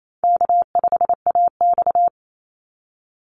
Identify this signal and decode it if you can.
Morse code, heard as a keyed tone, about 28 wpm, 700 Hz. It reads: K5AX
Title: Call Signs - 28wpm